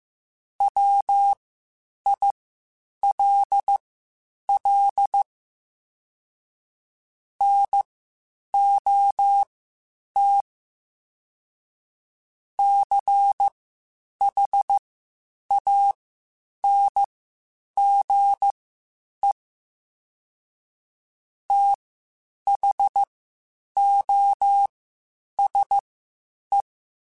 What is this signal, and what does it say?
Morse code, heard as a keyed tone, about 15 wpm, spaced as for 8 wpm, 800 Hz. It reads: WILL NOT CHANGE THOSE
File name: Quotes Spaced / Quotes Spaced-mp3s / IgnoringFacts sp @13